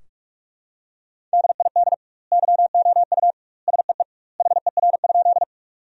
Morse code, heard as a keyed tone, about 41 wpm, 700 Hz. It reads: DID YOU SEE HER?
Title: 2x Word-Spacing - Sentences from Top 100 Words - 45wpm